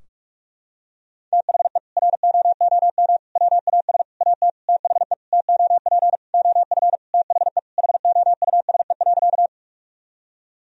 Morse code, heard as a keyed tone, about 47 wpm, 700 Hz. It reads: THE ROOM WAS AT THE TOP OF THE HOUSE.
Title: Sentences from Top 1000 Words - Review Part 2 - 45wpm